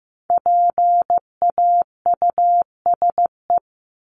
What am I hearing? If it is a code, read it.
Morse code, heard as a keyed tone, about 15 wpm, 700 Hz. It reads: PAUSE
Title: Top 1501-1750 Words 15wpm